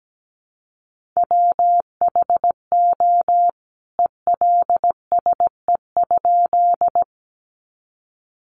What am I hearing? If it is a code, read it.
Morse code, heard as a keyed tone, about 17 wpm, 700 Hz. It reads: WHO ELSE?
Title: A Sound Of Thunder 17wpm